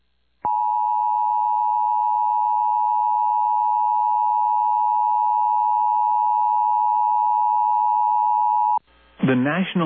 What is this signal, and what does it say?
Emergency Alert System (SAME) tones (the 853 and 960 Hz attention tone), with speech. It is heard from 0.4 to 8.8 s.
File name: eas / KUSO / KUSO